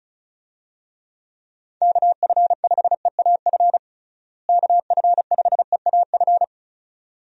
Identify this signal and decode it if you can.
Morse code, heard as a keyed tone, about 36 wpm, 700 Hz. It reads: KF5EAF KF5EAF
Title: USA Call Signs - 6 Characters - Repeated 2x - 35wpm